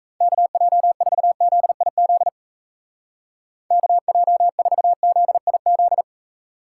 Morse code, heard as a keyed tone, about 36 wpm, 700 Hz. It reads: KJ4ZIZ KJ4ZIZ